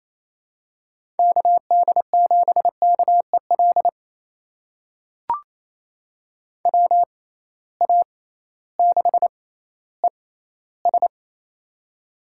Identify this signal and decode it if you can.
Morse code, heard as a keyed tone, about 28 wpm, spaced as for 15 wpm, 700 Hz. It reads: KD7KEL WA6ES
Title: USA Call Signs - 6x Character Spacing - 28wpm